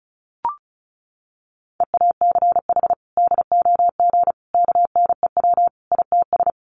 Morse code, heard as a keyed tone, about 34 wpm, 700 Hz. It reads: EACH DOG KNEW ITS
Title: Sentences from Top 1300 Words 35wpm